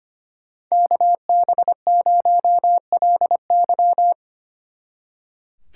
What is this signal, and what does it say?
Morse code, heard as a keyed tone, about 32 wpm, 700 Hz. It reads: KB0LY